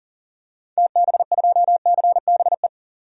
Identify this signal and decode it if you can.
Morse code, heard as a keyed tone, about 40 wpm, 700 Hz. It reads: TB2CBE
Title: DXCC-100 Call Signs - 6 Characters - 40wpm